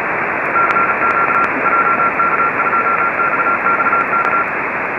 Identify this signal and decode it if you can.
Morse code, heard as a keyed tone, about 36 wpm, 1370 Hz, with radio noise. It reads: CQ JK2XXK